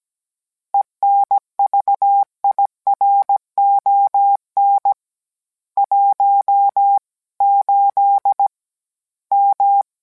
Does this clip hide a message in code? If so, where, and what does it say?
Yes: Morse code from 0.7 to 9.8 s, heard as a keyed tone, about 17 wpm, 800 Hz. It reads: ENVIRON 18 M